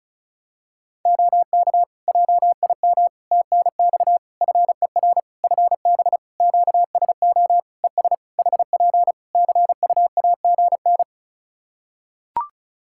Morse code, heard as a keyed tone, about 36 wpm, 700 Hz. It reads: OK JIM TNX FER FB QSO ES HP CUAGN
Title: Practice QSO - Exchange 4 - Ending - 35wpm